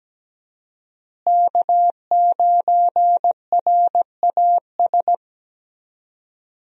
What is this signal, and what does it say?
Morse code, heard as a keyed tone, about 17 wpm, 700 Hz. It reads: K9RAS
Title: USA Call Signs - 5 Characters - Repeated 2x - 17wpm